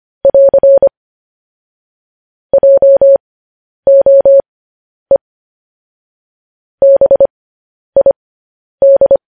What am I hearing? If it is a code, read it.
Morse code, heard as a keyed tone, about 25 wpm, spaced as for 10 wpm, 550 Hz. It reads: <AR> JOE BID
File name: podcast / News headlines-1x-25x10